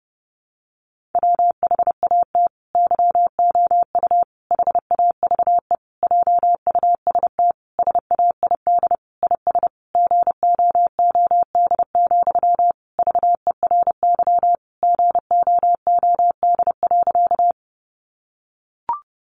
Morse code, heard as a keyed tone, about 30 wpm, 700 Hz. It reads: WHAT YOU HAVE JUST SAID IS GOOD, VERY GOOD.
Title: Sentences from Top 200 Words 30wpm